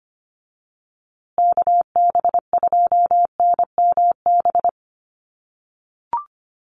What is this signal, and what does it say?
Morse code, heard as a keyed tone, about 25 wpm, 700 Hz. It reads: KB2NMB
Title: USA Call Signs Repeated 2x 25wpm